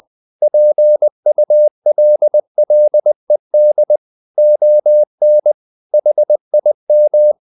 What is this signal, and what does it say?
Morse code, heard as a keyed tone, about 20 wpm, 600 Hz. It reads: PULLED ON HIM
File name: the old man and the sea part 2 20wpm